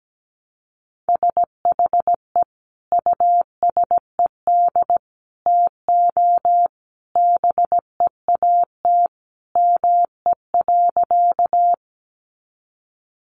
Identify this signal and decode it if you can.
Morse code, heard as a keyed tone, about 17 wpm, 700 Hz. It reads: SHE USED TO BEAT ME.
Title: Sentences from Top 1300 Words 17wpm